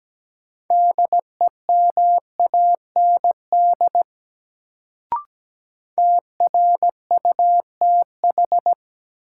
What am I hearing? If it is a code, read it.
Morse code, heard as a keyed tone, about 17 wpm, 700 Hz. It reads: DEMAND TRUTH
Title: Sentences from Top 800 Words - New Word Drill - 17wpm